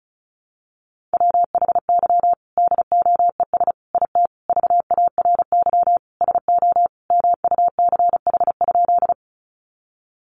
Morse code, heard as a keyed tone, about 34 wpm, 700 Hz. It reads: WHY DOES IT VARY SO MUCH?
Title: Sentences from Top 1200 Words 35wpm